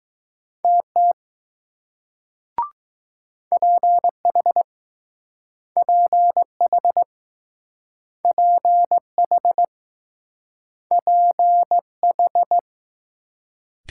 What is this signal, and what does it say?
Morse code, heard as a keyed tone, about 19 wpm, 700 Hz. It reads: TT PH PH PH PH